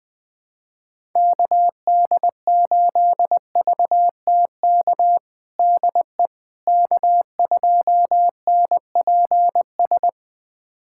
Morse code, heard as a keyed tone, about 20 wpm, 700 Hz. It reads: KD8VTK DE K2NPS
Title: USA Callsign DE Callsign 20wpm